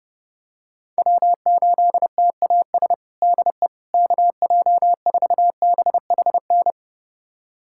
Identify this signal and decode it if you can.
Morse code, heard as a keyed tone, about 32 wpm, 700 Hz. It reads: W8TAS DE KJ4BHN
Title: USA Callsign DE Callsign 30wpm